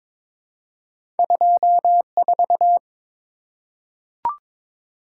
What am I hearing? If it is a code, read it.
Morse code, heard as a keyed tone, about 22 wpm, 700 Hz. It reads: 24
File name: practice / 22wpm / Two Letter-Number Combinations - 22wpm